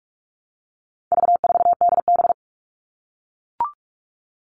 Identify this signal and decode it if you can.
Morse code, heard as a keyed tone, about 44 wpm, 700 Hz. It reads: V4DB
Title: All Cut Numbers - 4-Digit 45wpm